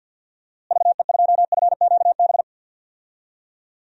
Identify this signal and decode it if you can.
Morse code, heard as a keyed tone, about 50 wpm, 700 Hz. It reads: VE2FQB
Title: DXCC-100 Call Signs - 6 Characters - Repeated 2x - 50wpm